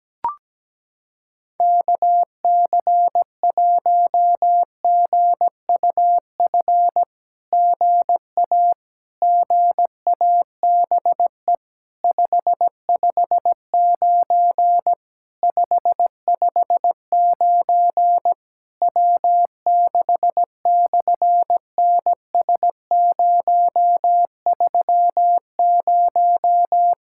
Morse code, heard as a keyed tone, about 17 wpm, 700 Hz. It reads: KC1GUF GA GABE 559 559 W6/NS030
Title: SOTA QSO Practice 17wpm